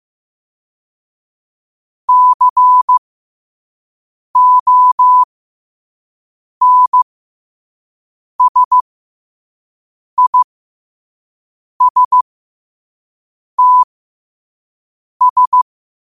Morse code, heard as a keyed tone, about 15 wpm, spaced as for 5 wpm, 1000 Hz. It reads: CONSISTS
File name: qotdfiles / 5wpm